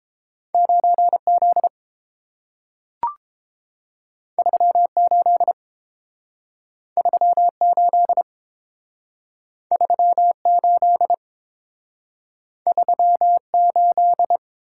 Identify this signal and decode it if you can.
Morse code, heard as a keyed tone, about 30 wpm, 700 Hz. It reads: 9Z 38 38 38 38